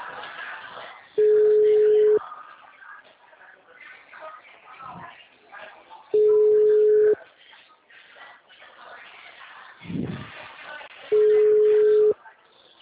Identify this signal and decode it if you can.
Ringback tone (Europe, 425 Hz), with speech.
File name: out- -2003- - - .16629